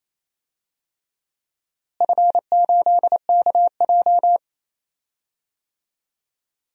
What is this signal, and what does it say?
Morse code, heard as a keyed tone, about 28 wpm, 700 Hz. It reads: F8KJ
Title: DXCC-100 Call Signs - 4 Characters - Repeated 3x - 28wpm